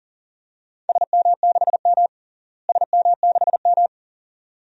Morse code, heard as a keyed tone, about 40 wpm, 700 Hz. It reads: SM6M SM6M
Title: DXCC-100 Call Signs - 4 Characters - Repeated 2x - 40wpm